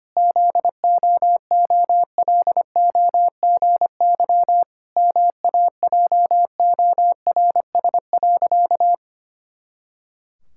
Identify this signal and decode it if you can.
Morse code, heard as a keyed tone, about 25 wpm, 700 Hz. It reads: ZOOLOGY MAJORS.
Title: Pangram Challenge v2 - 25wpm